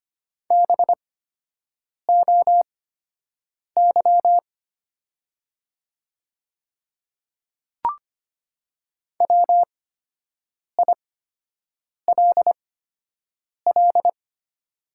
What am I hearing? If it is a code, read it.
Morse code, heard as a keyed tone, about 25 wpm, spaced as for 7 wpm, 700 Hz. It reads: BOY WILL
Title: 8x Character Spacing - Top 200 Words - 25wpm